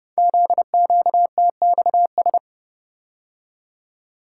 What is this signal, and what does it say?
Morse code, heard as a keyed tone, about 28 wpm, 700 Hz. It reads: ZQTXS
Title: Code Groups - 5 Letters - 30wpm